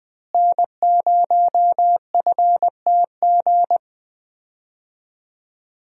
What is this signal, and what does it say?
Morse code, heard as a keyed tone, about 20 wpm, 700 Hz. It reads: N0FTG
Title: USA Call Signs - 5 Characters - Repeated 2x - 20wpm